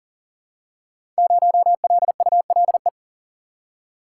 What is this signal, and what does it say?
Morse code, heard as a keyed tone, about 42 wpm, 700 Hz. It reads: 0LULE
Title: Code Groups - 5 Letter:Numbers - 40wpm